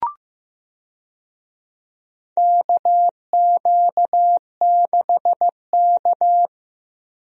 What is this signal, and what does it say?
Morse code, heard as a keyed tone, about 15 wpm, 700 Hz. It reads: KQ6K
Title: USA Call Signs - 4 Characters - Repeated 2x - 15wpm